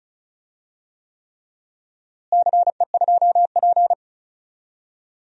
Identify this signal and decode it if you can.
Morse code, heard as a keyed tone, about 34 wpm, 700 Hz. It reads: CE2P
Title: DXCC-100 Call Signs - 4 Characters - Repeated 3x - 35wpm